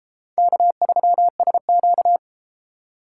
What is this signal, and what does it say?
Morse code, heard as a keyed tone, about 32 wpm, 700 Hz. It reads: K3SQ